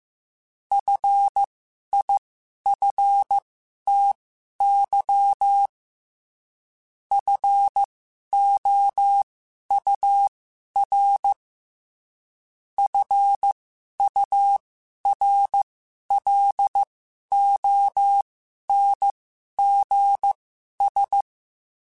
Morse code, heard as a keyed tone, about 15 wpm, spaced as for 11 wpm, 800 Hz. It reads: FIFTY FOUR FURLONGS
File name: F Runs / F Runs mp3s / F Runs @13